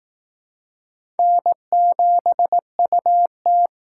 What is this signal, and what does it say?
Morse code, heard as a keyed tone, about 19 wpm, 700 Hz. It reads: N7UT
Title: Speed Racing - USA 1x2 Call Signs 15wpm